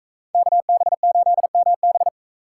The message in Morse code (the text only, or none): KB8MB